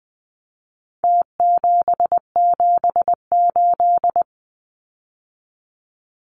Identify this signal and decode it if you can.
Morse code, heard as a keyed tone, about 20 wpm, 700 Hz. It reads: T778
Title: Numbers - Contest Style - 4-Digit 20wpm